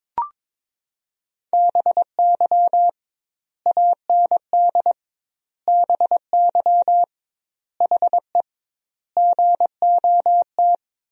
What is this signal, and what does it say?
Morse code, heard as a keyed tone, about 22 wpm, 700 Hz. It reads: BY AND BY HE GOT
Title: 2x Word-Spacing - Sentences from Top 300 Words - 22wpm